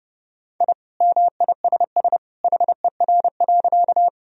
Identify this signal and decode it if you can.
Morse code, heard as a keyed tone, about 32 wpm, 700 Hz. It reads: I MISS HER.
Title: Sentences from Top 1000 Words - Review Part 3 - 30wpm